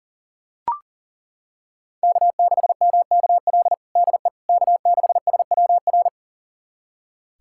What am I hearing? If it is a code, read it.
Morse code, heard as a keyed tone, about 37 wpm, 700 Hz. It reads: K6MKL DE K6SWR
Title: USA Callsign DE Callsign - 5 Characters - 40wpm